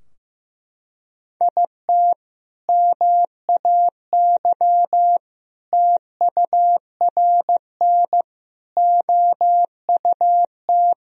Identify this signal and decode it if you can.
Morse code, heard as a keyed tone, about 15 wpm, 700 Hz. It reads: IT MAY TURN OUT
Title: Sentences from Top 1000 Words - Review Part 2 - 15wpm